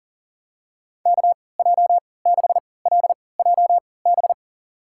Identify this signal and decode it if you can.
Morse code, heard as a keyed tone, about 40 wpm, spaced as for 23 wpm, 700 Hz. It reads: KJ6LJB
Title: USA Call Signs - 3x Character Spacing - 40wpm